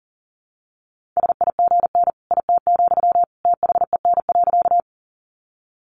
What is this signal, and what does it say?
Morse code, heard as a keyed tone, about 40 wpm, 700 Hz. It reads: SIGN IT, THEN.
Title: Sentences from Top 1000 Words 40wpm